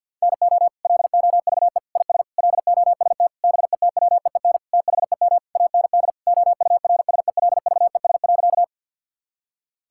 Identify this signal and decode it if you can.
Morse code, heard as a keyed tone, about 50 wpm, 700 Hz. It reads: NO LOVE IS LOST BETWEEN THEM AND OURSELVES.
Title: Sentences from Top 1600 Words 50wpm